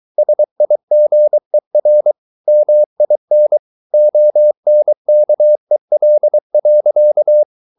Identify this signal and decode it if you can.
Morse code, heard as a keyed tone, about 23 wpm, 600 Hz. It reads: SIGER MIN ONKEL.
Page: Ferie 23 wpm